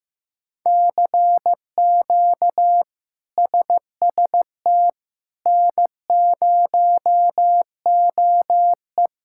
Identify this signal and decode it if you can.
Morse code, heard as a keyed tone, about 15 wpm, 700 Hz. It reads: CQ SST N0OE
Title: K1USN SST QSO Practice - 15wpm